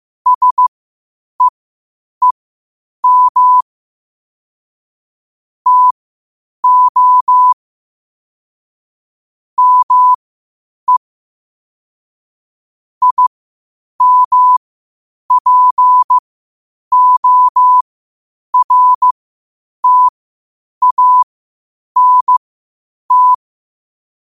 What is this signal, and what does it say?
Morse code, heard as a keyed tone, about 15 wpm, spaced as for 8 wpm, 1000 Hz. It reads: SEEM TO ME IMPORTANT
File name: qotdfiles / 8wpm